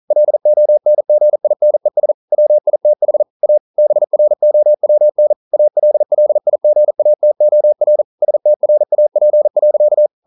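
Morse code, heard as a keyed tone, about 42 wpm, 600 Hz. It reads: LONGINES WITH A BROWN ALLIGATOR STRAP.
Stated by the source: Morse code practice at 41 wpm, from CW med Gnister.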